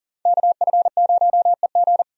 Morse code, heard as a keyed tone, about 40 wpm, 700 Hz. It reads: KF0EG
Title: USA Call Signs - 5 Characters - Repeated 2x - 40wpm